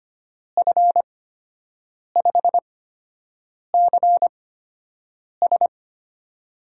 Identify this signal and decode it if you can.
Morse code, heard as a keyed tone, about 25 wpm, spaced as for 7 wpm, 700 Hz. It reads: F5CS
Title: USA Call Signs - 5 Characters - 8x Character Spacing - 25wpm